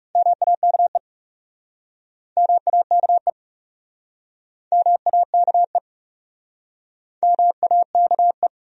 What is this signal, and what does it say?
Morse code, heard as a keyed tone, about 37 wpm, 700 Hz. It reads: MAKE MAKE MAKE MAKE
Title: Speed-Racing Top 100 Words 30wpm